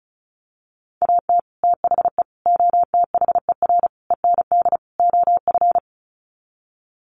Morse code, heard as a keyed tone, about 34 wpm, 700 Hz. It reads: AT THE OTHER END OF
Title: Sets of 5 Words 35wpm